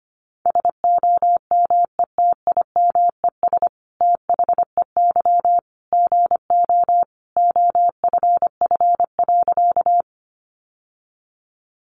Morse code, heard as a keyed tone, about 25 wpm, 700 Hz. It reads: SOMETIMES THEY GO OFF.
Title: Sentences from Top 600 Words 25wpm